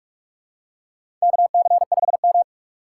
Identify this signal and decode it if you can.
Morse code, heard as a keyed tone, about 46 wpm, 700 Hz. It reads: KC5M
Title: USA Call Signs - 4 Characters - 6x Character Spacing - 45wpm